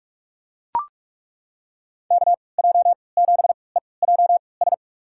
Morse code, heard as a keyed tone, about 42 wpm, spaced as for 25 wpm, 700 Hz. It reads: KJ7EJS
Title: USA Call Signs - 6 Characters - 3x Character Spacing - 45wpm